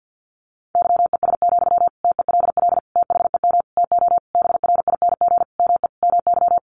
Morse code, heard as a keyed tone, about 50 wpm, 700 Hz. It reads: YES, TELL THEM TO BRING ME MY
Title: Sentences from Top 500 Words 50wpm